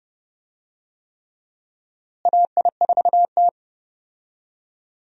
Morse code, heard as a keyed tone, about 30 wpm, 700 Hz. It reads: AI4T
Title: USA Call Signs - 4 Characters - Repeated 3x - 30wpm